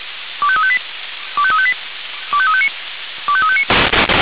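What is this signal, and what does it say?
Five-tone selective call (ZVEI), heard as quick runs of five tones, tones of about 70 ms. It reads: 25358 25358 25359 25359
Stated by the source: ZVEI five-tone selective call received with an SDR.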